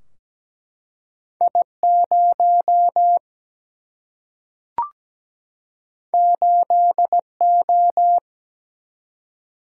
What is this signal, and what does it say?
Morse code, heard as a keyed tone, about 17 wpm, 700 Hz. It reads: I 0 8 O
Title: Single-Letter Single-Number 17wpm